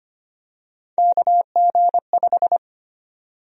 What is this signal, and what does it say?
Morse code, heard as a keyed tone, about 25 wpm, 700 Hz. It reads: KG5
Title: USA Call Sign Prefixes - 25wpm